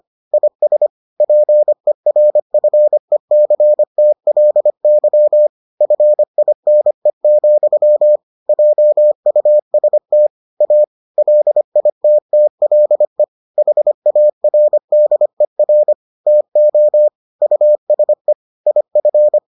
Morse code, heard as a keyed tone, about 25 wpm, 600 Hz. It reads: IS PERFECTLY FINE, JUST A LITTLE HARDER TO USE IF